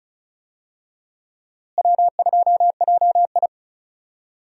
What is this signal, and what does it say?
Morse code, heard as a keyed tone, about 36 wpm, 700 Hz. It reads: W2JI
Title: USA Call Signs - 4 Characters - Repeated 2x - 35wpm